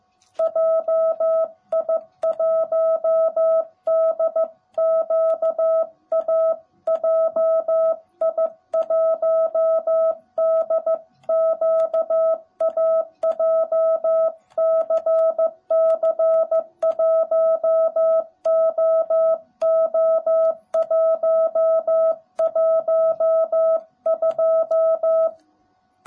Morse code, heard as a keyed tone, about 15 wpm, 660 Hz. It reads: JI1DQA JI1DQA JCC1OO112
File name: jh8cbh / 2021cw / cw18b